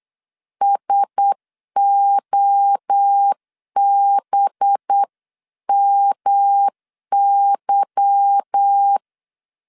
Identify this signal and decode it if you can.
Morse code, heard as a keyed tone, about 8 wpm, 800 Hz. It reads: SOBMY
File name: CW / DET / 1 9 1500C S C 8wpm 5653